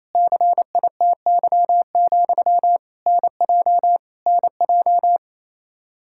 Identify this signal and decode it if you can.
Morse code, heard as a keyed tone, about 28 wpm, 700 Hz. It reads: CITY, NJ NJ
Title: Practice QSO - Exchange 1 - Beginning - 28wpm